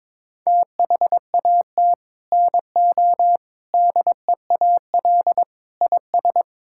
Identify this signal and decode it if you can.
Morse code, heard as a keyed tone, about 22 wpm, 700 Hz. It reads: THAT NO DEAL IS
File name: Sets of 9 Words - 22wpm